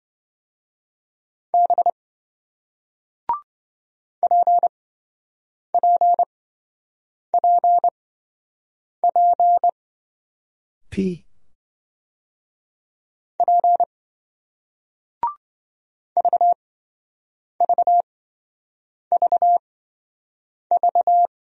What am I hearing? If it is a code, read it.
Morse code, heard as a keyed tone, about 27 wpm, spaced as for 7 wpm, 700 Hz. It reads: B PPPP P VVVV